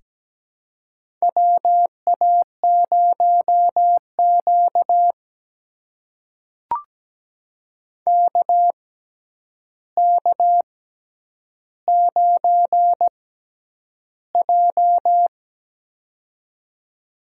Morse code, heard as a keyed tone, about 17 wpm, spaced as for 9 wpm, 700 Hz. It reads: WA0Q KK9J
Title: USA Call Signs - 4 Characters - 6x Character Spacing - 17wpm